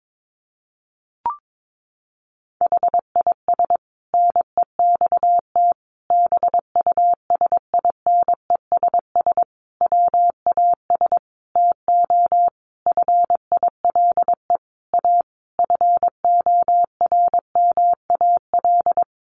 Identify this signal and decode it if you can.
Morse code, heard as a keyed tone, about 22 wpm, 700 Hz. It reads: HIS NEXT BUSINESS WAS TO FILE A FORMAL
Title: Sentences from Top 1300 Words 22wpm